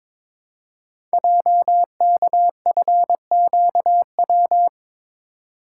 Morse code, heard as a keyed tone, about 22 wpm, 700 Hz. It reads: JKFQW
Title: Code Groups - 5 Letters - 22wpm